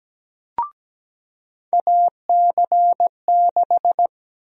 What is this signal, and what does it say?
Morse code, heard as a keyed tone, about 17 wpm, 700 Hz. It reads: AC6
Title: USA Call Sign Prefixes - 17wpm